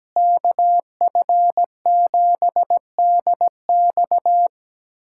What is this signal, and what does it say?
Morse code, heard as a keyed tone, about 17 wpm, 700 Hz. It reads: KF7DX
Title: USA Callsign DE Callsign - 5 Characters - 17wpm